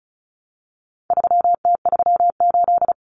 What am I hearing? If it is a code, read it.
Morse code, heard as a keyed tone, about 34 wpm, 700 Hz. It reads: 3T38
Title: Common Cut Numbers - 4-Digit 35wpm